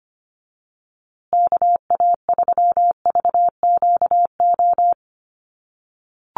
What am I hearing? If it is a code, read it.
Morse code, heard as a keyed tone, about 25 wpm, 700 Hz. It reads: KA3VQO
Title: USA Call Signs Repeated 2x 25wpm